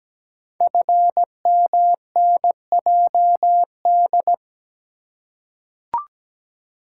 Morse code, heard as a keyed tone, about 17 wpm, 700 Hz. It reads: FMNJD
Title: Code Groups - 5 Letters - 17wpm